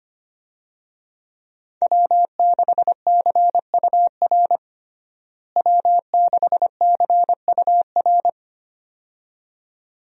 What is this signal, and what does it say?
Morse code, heard as a keyed tone, about 25 wpm, 700 Hz. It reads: W6CUR W6CUR
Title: USA Call Signs - 5 Characters - Repeated 2x - 25wpm